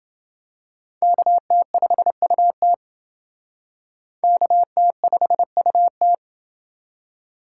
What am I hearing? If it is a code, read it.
Morse code, heard as a keyed tone, about 28 wpm, 700 Hz. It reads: KT5UT KT5UT